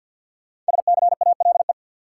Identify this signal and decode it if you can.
Morse code, heard as a keyed tone, about 51 wpm, 700 Hz. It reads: SCALE
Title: Sentences from Top 1000 Words - Review Part 4 - New Word Drill - 50wpm